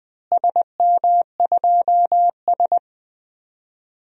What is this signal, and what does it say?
Morse code, heard as a keyed tone, about 20 wpm, 700 Hz. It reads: SM2S
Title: DXCC-100 Call Signs - 4 Characters - Repeated 2x - 20wpm